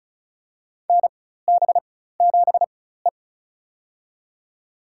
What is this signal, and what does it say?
Morse code, heard as a keyed tone, about 36 wpm, spaced as for 16 wpm, 700 Hz. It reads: NB7E
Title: USA Call Signs - 4 Characters - 4x Character Spacing - 35wpm